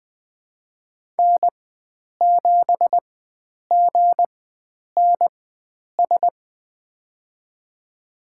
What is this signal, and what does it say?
Morse code, heard as a keyed tone, about 20 wpm, spaced as for 9 wpm, 700 Hz. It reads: N7GNS
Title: USA Call Signs - 5 Characters - 4x Character Spacing - 20wpm